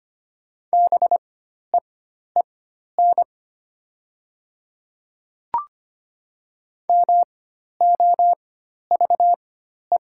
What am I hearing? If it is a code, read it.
Morse code, heard as a keyed tone, about 25 wpm, spaced as for 12 wpm, 700 Hz. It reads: BEEN MOVE
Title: 4x Character Spacing - Top 300 Words - 25wpm